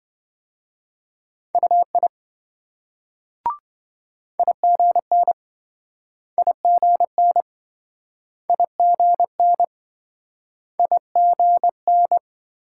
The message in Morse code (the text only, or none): UI IGN IGN IGN IGN